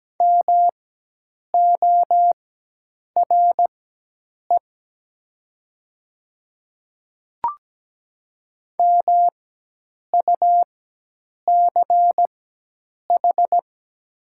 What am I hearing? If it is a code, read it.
Morse code, heard as a keyed tone, about 17 wpm, spaced as for 8 wpm, 700 Hz. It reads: MORE MUCH
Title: 4x Character Spacing - Top 300 Words - 17wpm